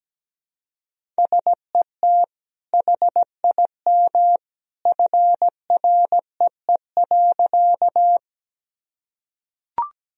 Morse code, heard as a keyed tone, about 17 wpm, 700 Hz. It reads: SET HIM FREE.
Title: Sentences from Top 1000 Words - Review Part 1 - 17wpm